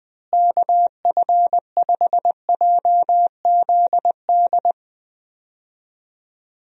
Morse code, heard as a keyed tone, about 20 wpm, 700 Hz. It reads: KF5JZD
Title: USA Call Signs - 6 Characters - Repeated 2x - 20wpm